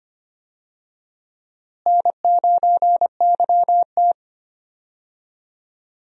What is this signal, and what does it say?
Morse code, heard as a keyed tone, about 25 wpm, 700 Hz. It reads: N9YT
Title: USA Call Signs - 4 Characters - Repeated 2x - 25wpm